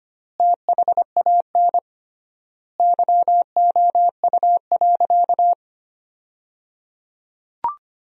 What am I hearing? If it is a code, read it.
Morse code, heard as a keyed tone, about 25 wpm, 700 Hz. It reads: THAN YOU.
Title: 3x Word-Spacing - Sentences from Top 400 Words - 25wpm